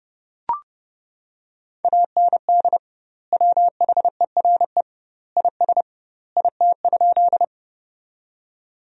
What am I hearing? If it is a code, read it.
Morse code, heard as a keyed tone, about 29 wpm, 700 Hz. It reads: AND WHERE IS IT?
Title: 2x Word-Spacing - Sentences from Top 200 Words - 30wpm